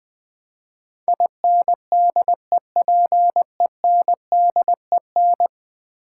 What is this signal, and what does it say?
Morse code, heard as a keyed tone, about 21 wpm, 700 Hz. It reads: INDEPENDEN
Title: Build A Word - 5 Letters At A Time - 12-Letter Words - 20wpm